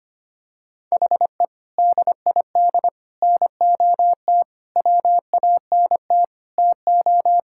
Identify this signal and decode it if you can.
Morse code, heard as a keyed tone, about 25 wpm, 700 Hz. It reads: HE DID NOT WANT TO
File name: practice / 25wpm / Sentences with 3-Word Phrases - 25wpm